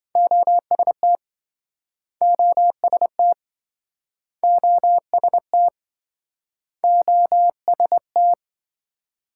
Morse code, heard as a keyed tone, about 27 wpm, 700 Hz. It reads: OST OST OST OST